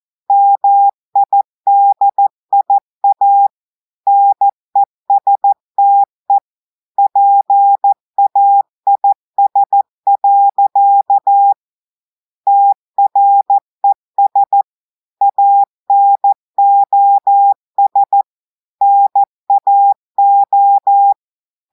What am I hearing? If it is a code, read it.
Morse code, heard as a keyed tone, about 14 wpm, 800 Hz. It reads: MIDIA NESTE PAIS. TRES ANOS NAO
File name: audios / portugais / portugais texte en clair 1 vitesse 14 mots minutes